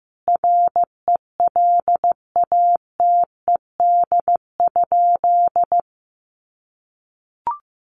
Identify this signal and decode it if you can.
Morse code, heard as a keyed tone, about 15 wpm, 700 Hz. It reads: RELATED?
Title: Sentences from Top 1400 Words 15wpm